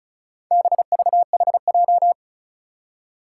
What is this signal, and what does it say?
Morse code, heard as a keyed tone, about 36 wpm, 700 Hz. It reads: BVHJ
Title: Four Challenging Letter-Number Combinations 35wpm